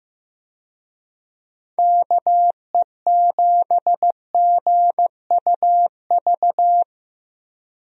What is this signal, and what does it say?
Morse code, heard as a keyed tone, about 15 wpm, 700 Hz. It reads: KE7GUV
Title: USA Call Signs - 6 Characters - Repeated 2x - 15wpm